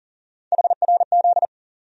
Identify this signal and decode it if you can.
Morse code, heard as a keyed tone, about 40 wpm, 700 Hz. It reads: HRZ
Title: USA Call Sign Postfixes - 3 Letters 40wpm